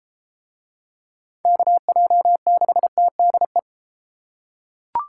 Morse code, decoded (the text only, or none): KJ6TDE